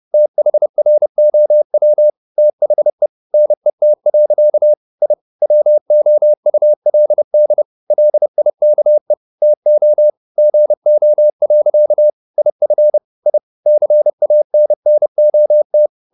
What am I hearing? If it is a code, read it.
Morse code, heard as a keyed tone, about 30 wpm, 600 Hz. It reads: THROW THE NET. I WOULD LIKE TO GO. IF I CANNOT
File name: the old man and the sea / the old man and the sea 1 30wpm